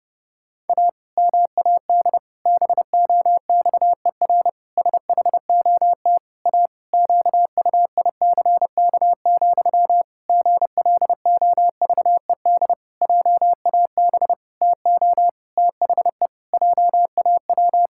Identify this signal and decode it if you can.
Morse code, heard as a keyed tone, about 32 wpm, 700 Hz. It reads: A MAD BOXER SHOT A QUICK, GLOVED JAB TO THE JAW
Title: Pangram Challenge v2 - 30wpm